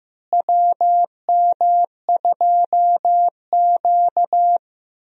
Morse code, heard as a keyed tone, about 15 wpm, 700 Hz. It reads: WM2Q
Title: USA Call Signs - 4 Characters - Repeated 2x - 15wpm